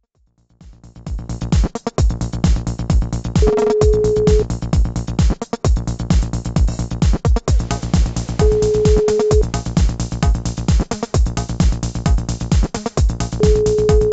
Ringback tone (Europe, 425 Hz), with noise.